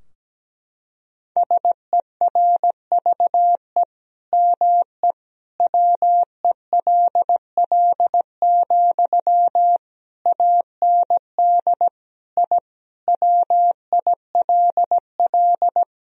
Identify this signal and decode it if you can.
Morse code, heard as a keyed tone, about 17 wpm, 700 Hz. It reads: SERVE ME WELL, AND I WILL
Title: Sentences from Top 1000 Words - Review Part 2 - 17wpm